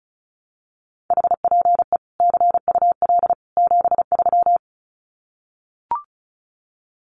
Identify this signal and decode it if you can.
Morse code, heard as a keyed tone, about 34 wpm, 700 Hz. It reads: HPE CUL 73
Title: Common QSO Phrases 35wpm